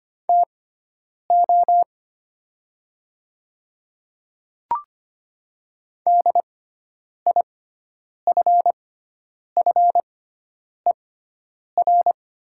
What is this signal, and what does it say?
Morse code, heard as a keyed tone, about 25 wpm, spaced as for 9 wpm, 700 Hz. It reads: TO DIFFER
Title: 6x Character Spacing - Top 200 Words - 25wpm